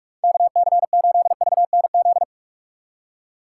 Morse code, heard as a keyed tone, about 44 wpm, 700 Hz. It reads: KC8VNZ
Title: USA Call Signs - 6 Characters - 6x Character Spacing - 45wpm